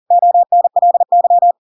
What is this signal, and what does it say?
Morse code, heard as a keyed tone, about 40 wpm, 700 Hz. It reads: ONLY
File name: Pinocchio / Pinocchio ch5 40 40 ik1hge